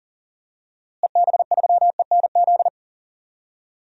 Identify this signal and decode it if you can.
Morse code, heard as a keyed tone, about 40 wpm, 700 Hz. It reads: EB3ENZ
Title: DXCC-100 Call Signs - 6 Characters - Repeated 2x - 40wpm